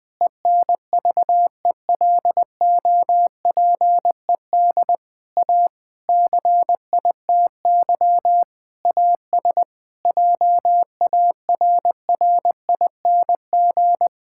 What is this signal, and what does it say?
Morse code, heard as a keyed tone, about 20 wpm, 700 Hz. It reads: ENVELOPED A CITY AS JARRING
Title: Pangram Challenge v2 - 20wpm